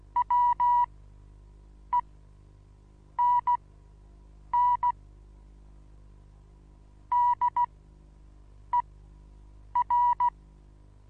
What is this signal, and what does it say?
Morse code, heard as a keyed tone, about 16 wpm, spaced as for 7 wpm, 970 Hz. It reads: WENN DER
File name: Onlinelehrgang / morsen / DJ4UF Morsen25